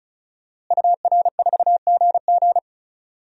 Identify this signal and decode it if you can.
Morse code, heard as a keyed tone, about 34 wpm, 700 Hz. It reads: UR4GG
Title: DXCC-100 Call Signs - 5 Characters - Repeated 3x - 35wpm